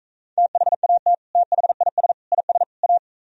Morse code, heard as a keyed tone, about 45 wpm, 700 Hz. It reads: THAT THIS IS A